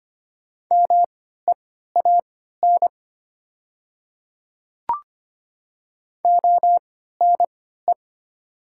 Morse code, heard as a keyed tone, about 25 wpm, spaced as for 14 wpm, 700 Hz. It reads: MEAN ONE
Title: 3x Character Spacing - Top 500 Words - 25wpm